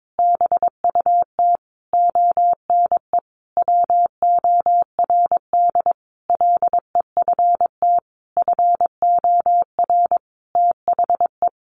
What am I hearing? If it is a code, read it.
Morse code, heard as a keyed tone, about 22 wpm, 700 Hz. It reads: BUT ONE WORD LEFT FOR THE
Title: Sentences from Top 1500 Words 22wpm